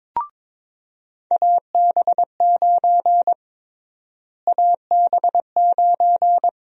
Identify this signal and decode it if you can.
Morse code, heard as a keyed tone, about 22 wpm, 700 Hz. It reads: AB9 AB9
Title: USA Call Sign Prefixes - Repeated 2x - 22wpm